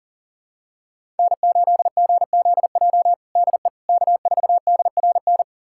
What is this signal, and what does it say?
Morse code, heard as a keyed tone, about 37 wpm, 700 Hz. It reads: N8GZJ DE K4DRN
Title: USA Callsign DE Callsign 40wpm